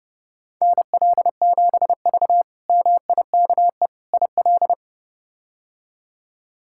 Morse code, heard as a keyed tone, about 30 wpm, 700 Hz. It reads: NL7V MIKE IL
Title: K1USN SST QSO Practice - 30wpm